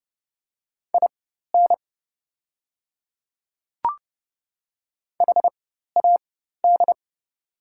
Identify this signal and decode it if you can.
Morse code, heard as a keyed tone, about 30 wpm, spaced as for 14 wpm, 700 Hz. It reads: IN HAD
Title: 4x Character Spacing - Top 200 Words - 30wpm